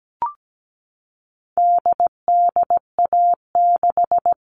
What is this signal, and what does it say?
Morse code, heard as a keyed tone, about 17 wpm, 700 Hz. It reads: DDA6
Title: All Cut Numbers - 4-Digit 17wpm